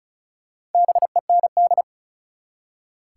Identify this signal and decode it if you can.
Morse code, heard as a keyed tone, about 36 wpm, 700 Hz. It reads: BEND
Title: Four Letter Words - 35wpm